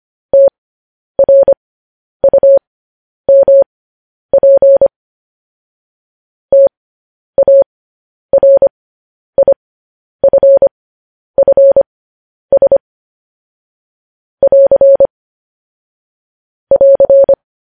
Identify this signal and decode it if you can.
Morse code, heard as a keyed tone, about 25 wpm, spaced as for 10 wpm, 550 Hz. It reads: TRUMP TARIFFS <AR> <AR>
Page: news headlines in morse code at 10 wpm